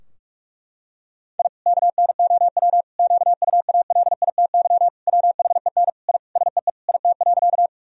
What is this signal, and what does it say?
Morse code, heard as a keyed tone, about 46 wpm, 700 Hz. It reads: I KNOW QUALITY WHEN I SEE IT.
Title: Sentences from Top 1000 Words - Review Part 4 - 45wpm